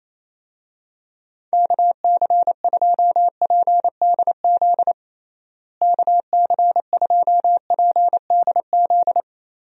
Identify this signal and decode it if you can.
Morse code, heard as a keyed tone, about 28 wpm, 700 Hz. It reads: KC2PDZ KC2PDZ
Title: USA Call Signs - 6 Characters - Repeated 3x - 28wpm